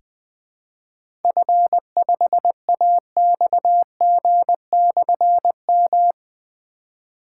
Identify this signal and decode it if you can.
Morse code, heard as a keyed tone, about 20 wpm, 700 Hz. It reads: F5AXG/M
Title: Difficult Call Signs 20wpm